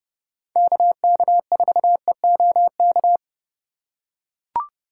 Morse code, heard as a keyed tone, about 29 wpm, 700 Hz. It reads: KK4EOK
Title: USA Call Signs - 3x Character Spacing - 30wpm